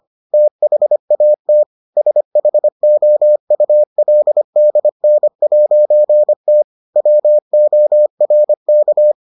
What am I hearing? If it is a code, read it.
Morse code, heard as a keyed tone, about 25 wpm, 600 Hz. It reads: THAT SHOULDN'T WORK